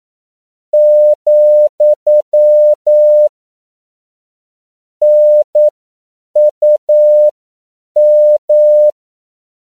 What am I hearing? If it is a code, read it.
Morse code, heard as a keyed tone, about 9 wpm, spaced as for 7 wpm, 600 Hz. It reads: , NUM
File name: CW / 10PPM